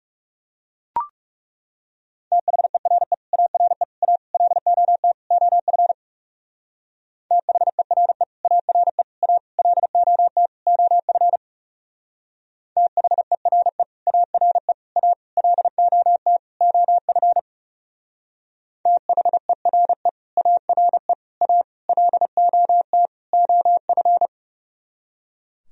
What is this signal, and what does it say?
Morse code, heard as a keyed tone, about 34 wpm, 700 Hz. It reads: THERE ARE A LOT OF THERE ARE A LOT OF THERE ARE A LOT OF THERE ARE A LOT OF
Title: Speed-Racing - Sets of 5 Words 30wpm